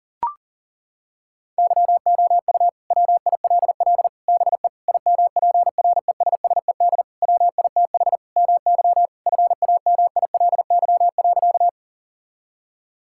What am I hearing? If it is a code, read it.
Morse code, heard as a keyed tone, about 40 wpm, 700 Hz. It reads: YOU WILL BE IMPRESSED WITH MY FAMILY.
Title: Sentences from Top 1600 Words 40wpm